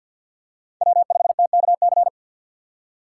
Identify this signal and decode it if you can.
Morse code, heard as a keyed tone, about 52 wpm, 700 Hz. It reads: W5TKC
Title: USA Call Signs - 5 Characters - 6x Character Spacing - 50wpm